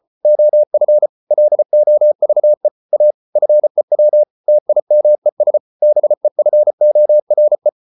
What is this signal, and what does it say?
Morse code, heard as a keyed tone, about 34 wpm, 600 Hz. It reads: OF LOVE A FEW TIMES BEFORE